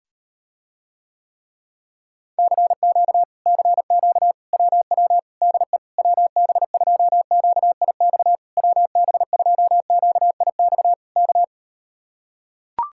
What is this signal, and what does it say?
Morse code, heard as a keyed tone, about 38 wpm, 700 Hz. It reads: CQ CQ WW DE WB2QIX WB2QIX K